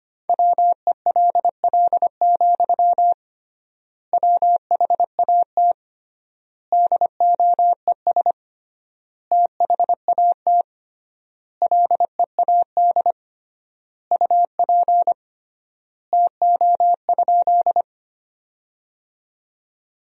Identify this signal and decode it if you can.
Morse code, heard as a keyed tone, about 25 wpm, 700 Hz. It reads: WELL, WHAT DOES THAT LEAD UP TO?
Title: 3x Word-Spacing - Sentences from Top 500 Words - 25wpm